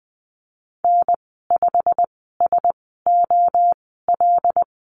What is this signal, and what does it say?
Morse code, heard as a keyed tone, about 20 wpm, spaced as for 15 wpm, 700 Hz. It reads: N5SOL
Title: 088 Lesson 15g - Number 5 - Call Signs 20wpm